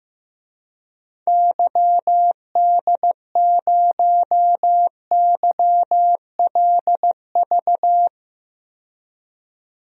Morse code, heard as a keyed tone, about 15 wpm, 700 Hz. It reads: YD0YLV
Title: DXCC-100 Call Signs - 6 Characters - 15wpm